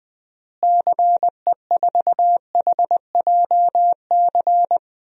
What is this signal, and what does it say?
Morse code, heard as a keyed tone, about 20 wpm, 700 Hz. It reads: CE4HJC
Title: DXCC-100 Call Signs - 6 Characters - Repeated 2x - 20wpm